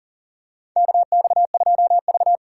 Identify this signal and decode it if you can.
Morse code, heard as a keyed tone, about 40 wpm, 700 Hz. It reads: KX2V
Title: USA Call Signs - 3x Character Spacing - 40wpm